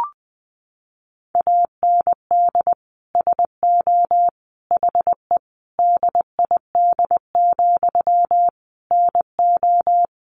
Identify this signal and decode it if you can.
Morse code, heard as a keyed tone, about 20 wpm, 700 Hz. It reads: AND SO HE DID, NO
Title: Sentences from Top 1500 Words 20wpm